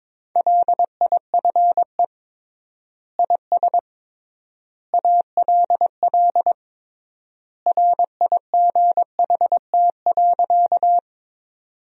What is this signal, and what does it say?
Morse code, heard as a keyed tone, about 22 wpm, 700 Hz. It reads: LIFE IS ALL RIGHT.
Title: 3x Word-Spacing - Sentences from Top 300 Words - 22wpm